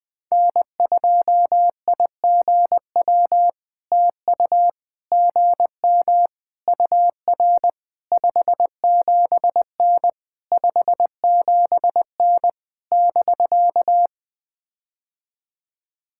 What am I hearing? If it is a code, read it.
Morse code, heard as a keyed tone, about 21 wpm, 700 Hz. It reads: N2IGW TU GM UR 57N 57N <BK>
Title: POTA QSO Practice - 20wpm